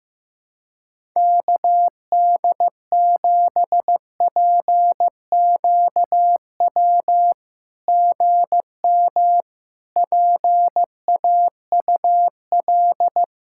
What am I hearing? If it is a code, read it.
Morse code, heard as a keyed tone, about 15 wpm, 700 Hz. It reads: KD7PQW GM PAUL
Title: SOTA QSO Practice 15wpm